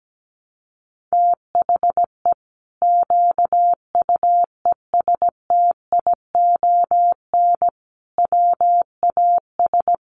Morse code, heard as a keyed tone, about 17 wpm, 700 Hz. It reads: THE QUESTION WAS
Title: Sentences from Top 1200 Words 17wpm